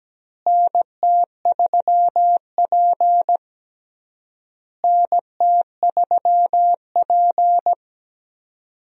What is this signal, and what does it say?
Morse code, heard as a keyed tone, about 17 wpm, 700 Hz. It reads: NT3P NT3P
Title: USA Call Signs - 4 Characters - Repeated 3x - 17wpm